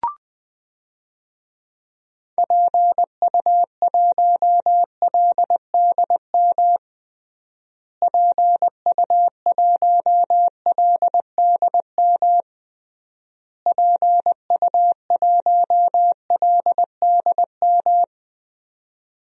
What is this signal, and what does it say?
Morse code, heard as a keyed tone, about 20 wpm, 700 Hz. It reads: PU1LDM PU1LDM PU1LDM
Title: DXCC-100 Call Signs - 6 Characters - Repeated 3x - 20wpm